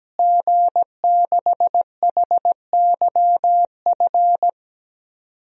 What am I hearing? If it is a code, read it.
Morse code, heard as a keyed tone, about 17 wpm, 700 Hz. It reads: G6HYF
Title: DXCC-100 Call Signs - 5 Characters - Repeated 3x - 17wpm